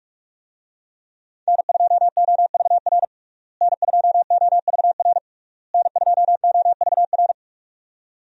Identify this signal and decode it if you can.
Morse code, heard as a keyed tone, about 42 wpm, 700 Hz. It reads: N2OVR N2OVR N2OVR
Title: USA Call Signs - 5 Characters - Repeated 3x - 45wpm